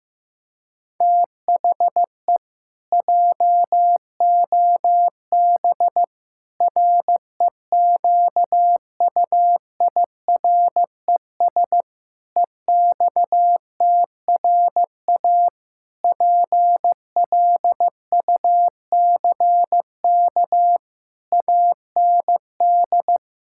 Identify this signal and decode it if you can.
Morse code, heard as a keyed tone, about 15 wpm, 700 Hz. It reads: THE JOB REQUIRES EXTRA PLUCK AND
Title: Pangram Challenge v2 - 15wpm